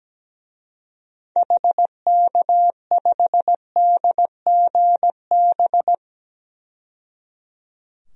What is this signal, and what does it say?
Morse code, heard as a keyed tone, about 17 wpm, 700 Hz. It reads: HK5DGB
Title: DXCC-100 Call Signs - 6 Characters - 17wpm